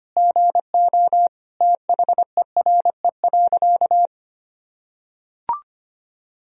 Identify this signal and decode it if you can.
Morse code, heard as a keyed tone, about 25 wpm, 700 Hz. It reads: GO THERE.
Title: Sentences from Top 1000 Words - Review Part 1 - 25wpm